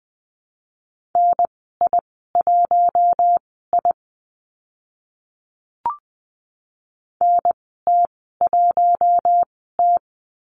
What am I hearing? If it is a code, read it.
Morse code, heard as a keyed tone, about 20 wpm, spaced as for 14 wpm, 700 Hz. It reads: NI1I NT1T
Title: 042 Lesson 8g - Number 1 - Call Signs 20wpm